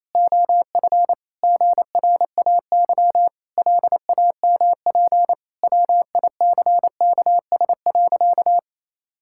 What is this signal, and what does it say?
Morse code, heard as a keyed tone, about 28 wpm, 700 Hz. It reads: OF GRAY LAMP WICKS.
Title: Pangram Challenge v2 - 28wpm